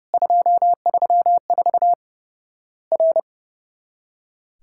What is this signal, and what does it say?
Morse code, heard as a keyed tone, about 30 wpm, 700 Hz. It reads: 234 R
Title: Worked All Europe DX Contest - QTC Exchange - 30wpm